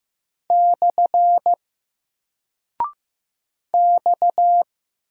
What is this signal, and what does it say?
Morse code, heard as a keyed tone, about 15 wpm, 700 Hz. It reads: / X
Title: Single Character (Letter:Number:Punctuation:Prosign) 15wpm